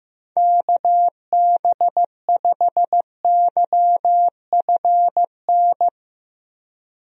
Morse code, heard as a keyed tone, about 15 wpm, 700 Hz. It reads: KB5YFN
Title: USA Call Signs - 6 Characters - 15wpm